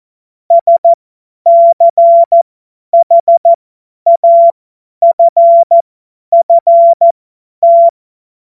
Morse code, heard as a keyed tone, about 14 wpm, spaced as for 10 wpm, 680 Hz. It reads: SCHAFFT
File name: Morsekurs / Witze / Witz 24.Oktober 2023 50 F70